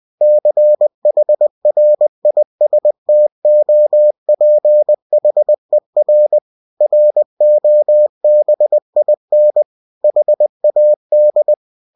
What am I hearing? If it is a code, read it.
Morse code, heard as a keyed tone, about 20 wpm, 600 Hz. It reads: CHRISTOPHER ROBIN HAD
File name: cw-books / winnie-the-pooh / winnie-20wpm-page-0106